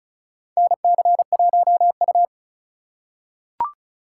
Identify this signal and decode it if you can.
Morse code, heard as a keyed tone, about 36 wpm, 700 Hz. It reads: NC1U
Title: USA Call Signs - 4 Characters - 4x Character Spacing - 35wpm